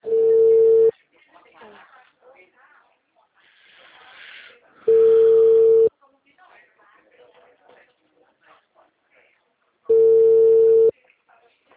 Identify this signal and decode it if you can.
Ringback tone (China, 450 Hz), with speech.
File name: out- -2023